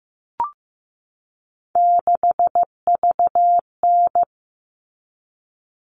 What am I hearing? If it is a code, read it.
Morse code, heard as a keyed tone, about 15 wpm, 700 Hz. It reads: 6VN
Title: All Cut Numbers - 3-Digit 15wpm